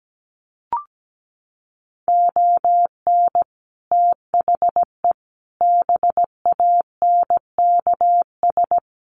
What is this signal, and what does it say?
Morse code, heard as a keyed tone, about 17 wpm, 700 Hz. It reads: ON THE BANKS
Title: Sets of 5 Words Encore2 17wpm